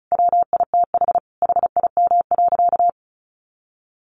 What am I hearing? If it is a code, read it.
Morse code, heard as a keyed tone, about 34 wpm, 700 Hz. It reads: WITH HIM.
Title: Sentences from Top 400 Words 35wpm